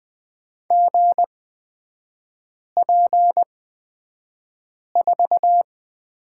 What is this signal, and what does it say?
Morse code, heard as a keyed tone, about 20 wpm, spaced as for 5 wpm, 700 Hz. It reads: GP4
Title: 224 Lesson 36e - BT - Three Characters 20wpm